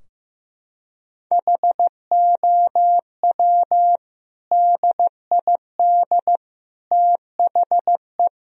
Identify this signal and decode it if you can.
Morse code, heard as a keyed tone, about 15 wpm, 700 Hz. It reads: HOW DID THE
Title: Sentences from Top 1000 Words - Review Part 4 - 15wpm